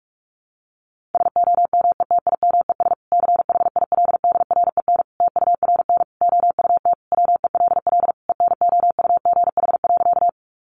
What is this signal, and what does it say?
Morse code, heard as a keyed tone, about 44 wpm, 700 Hz. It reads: SOMETIMES CHILDREN TURN OUT WELL ENOUGH.
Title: Sentences from Top 600 Words 45wpm